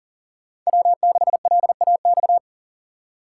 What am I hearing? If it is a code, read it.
Morse code, heard as a keyed tone, about 40 wpm, 700 Hz. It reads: W6LAX
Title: K1USN SST QSO Practice - 40wpm